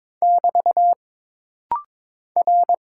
Morse code, heard as a keyed tone, about 22 wpm, 700 Hz. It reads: <BT> R
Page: This rapid-fire timing is geared to push your ICR (Instant Character Recognition) skills.